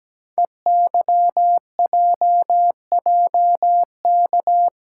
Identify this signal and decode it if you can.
Morse code, heard as a keyed tone, about 17 wpm, 700 Hz. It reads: EYJJK
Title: Code Groups - 5 Letters - 17wpm